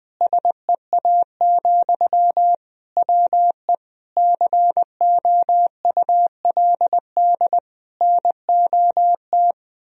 Morse code, heard as a keyed tone, about 20 wpm, 700 Hz. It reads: SEA, WE COULD NOT
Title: Sentences from Top 1000 Words - Review Part 2 - 20wpm